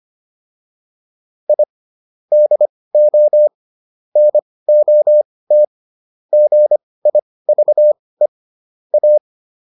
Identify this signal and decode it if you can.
Morse code, heard as a keyed tone, about 25 wpm, spaced as for 18 wpm, 600 Hz. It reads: I DO NOT GIVE A